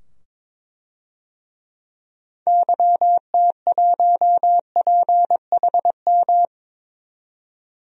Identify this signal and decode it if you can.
Morse code, heard as a keyed tone, about 22 wpm, 700 Hz. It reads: YT1PHM
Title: DXCC-100 Call Signs - 6 Characters - 22wpm